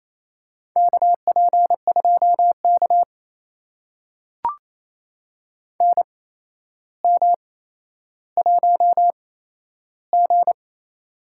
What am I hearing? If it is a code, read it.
Morse code, heard as a keyed tone, about 28 wpm, spaced as for 12 wpm, 700 Hz. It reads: KP2K NM1G
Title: USA Call Signs - 4 Characters - 8x Character Spacing - 28wpm